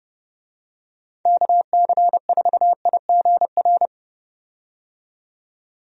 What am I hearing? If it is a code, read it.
Morse code, heard as a keyed tone, about 30 wpm, 700 Hz. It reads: KC4IGR
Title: USA Call Signs - 6 Characters - 30wpm